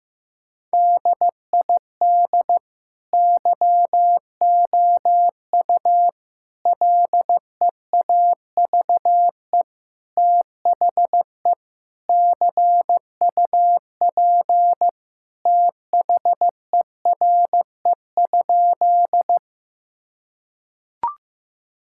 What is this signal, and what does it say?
Morse code, heard as a keyed tone, about 15 wpm, 700 Hz. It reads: DID YOU LEAVE THE CUP THERE?
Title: Sentences from Top 1600 Words 15wpm